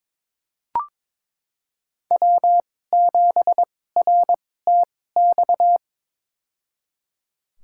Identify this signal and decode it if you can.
Morse code, heard as a keyed tone, about 21 wpm, spaced as for 16 wpm, 700 Hz. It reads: W7RTX
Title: USA Call Signs - 5 Characters - 2x Character Spacing - 22wpm